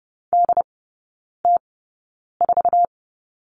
Morse code, heard as a keyed tone, about 30 wpm, spaced as for 9 wpm, 700 Hz. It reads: DT4
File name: Lesson 15e - Number 5 - Three Characters 30wpm